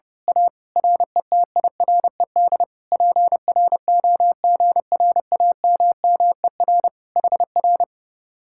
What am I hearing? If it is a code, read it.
Morse code, heard as a keyed tone, about 28 wpm, 700 Hz. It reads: A RETIRED PROGRAMMER HR
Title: Speed-Racing Common QSO Phrases - 30wpm